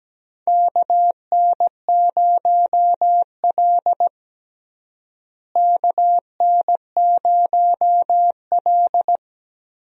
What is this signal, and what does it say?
Morse code, heard as a keyed tone, about 17 wpm, 700 Hz. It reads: KN0L KN0L
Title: USA Call Signs - 4 Characters - Repeated 2x - 17wpm